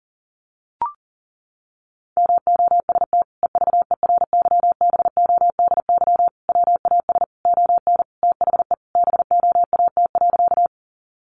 Morse code, heard as a keyed tone, about 40 wpm, 700 Hz. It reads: MOST EVERYBODY WAS ON THE BOAT.
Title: Sentences from Top 1400 Words 40wpm